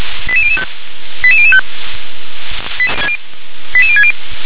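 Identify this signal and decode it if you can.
Five-tone selective call (ZVEI), heard as quick runs of five tones, tones of about 70 ms. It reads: 88005 88005 88060
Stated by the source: ZVEI five-tone selective call 88740, received by SDR.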